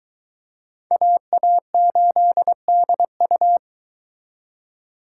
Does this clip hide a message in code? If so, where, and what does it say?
Yes: Morse code from 0.9 to 3.6 s, heard as a keyed tone, about 22 wpm, 700 Hz. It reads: AA8DU